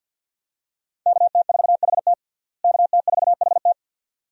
Morse code, heard as a keyed tone, about 52 wpm, 700 Hz. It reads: KT4HT KT4HT
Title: USA Call Signs - 5 Characters - Repeated 2x - 50wpm